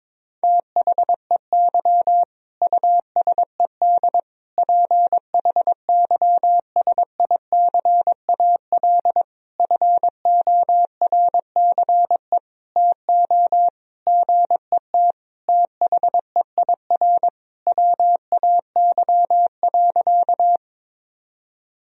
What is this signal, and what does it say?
Morse code, heard as a keyed tone, about 22 wpm, 700 Hz. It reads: THEY USED PHYSICAL FORCE TO GET THEIR WAY.
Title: Sentences from Top 1000 Words - Review Part 4 - 22wpm